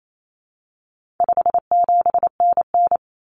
Morse code, heard as a keyed tone, about 28 wpm, 700 Hz. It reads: 57NN
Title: Common Cut Numbers - 4-Digit 28wpm